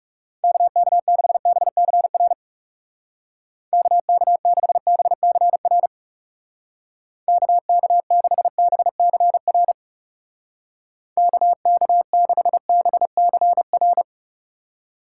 Morse code, heard as a keyed tone, about 38 wpm, 700 Hz. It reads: KK6BCR KK6BCR KK6BCR KK6BCR